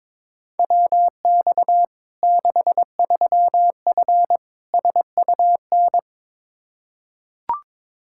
Morse code, heard as a keyed tone, about 21 wpm, 700 Hz. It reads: WX 63F SUN
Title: Practice QSO - Exchange 2 - Info - 22wpm